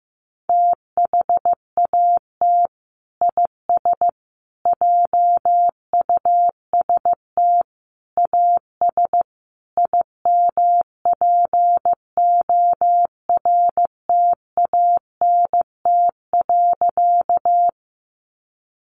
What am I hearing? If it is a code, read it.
Morse code, heard as a keyed tone, about 15 wpm, 700 Hz. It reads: THAT IS JUST AS IMPORTANT.
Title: Sentences from Top 600 Words 15wpm